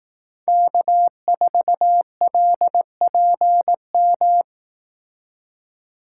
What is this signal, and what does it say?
Morse code, heard as a keyed tone, about 18 wpm, 700 Hz. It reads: K4LPM
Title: Speed Racing - USA 1x3 Call Signs 15wpm